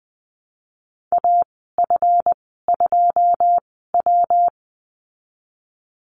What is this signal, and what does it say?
Morse code, heard as a keyed tone, about 20 wpm, spaced as for 15 wpm, 700 Hz. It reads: AF2W
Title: 161 Lesson 26g - Letter g - Call Signs 20wpm